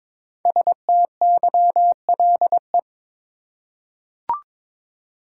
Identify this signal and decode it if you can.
Morse code, heard as a keyed tone, about 22 wpm, 700 Hz. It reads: STYLE
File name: 5 Letter Words - First 3 Letters Given - 22wpm